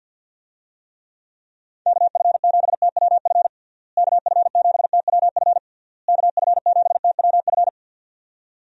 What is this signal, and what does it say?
Morse code, heard as a keyed tone, about 50 wpm, 700 Hz. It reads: KF7TWF KF7TWF KF7TWF
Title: USA Call Signs - 6 Characters - Repeated 3x - 50wpm